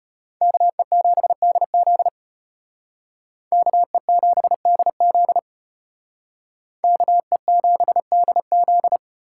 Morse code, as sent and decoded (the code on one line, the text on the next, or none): -.- . --... -.. --.. / -.- . --... -.. --.. / -.- . --... -.. --..
KE7DZ KE7DZ KE7DZ